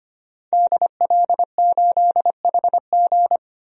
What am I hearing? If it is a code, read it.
Morse code, heard as a keyed tone, about 25 wpm, 700 Hz. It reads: DL8HG
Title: DXCC-100 Call Signs - 5 Characters - 25wpm